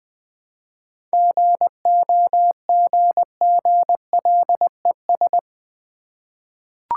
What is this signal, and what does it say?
Morse code, heard as a keyed tone, about 20 wpm, 700 Hz. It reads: GOGGLES
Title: Easy Words with Hints - 20wpm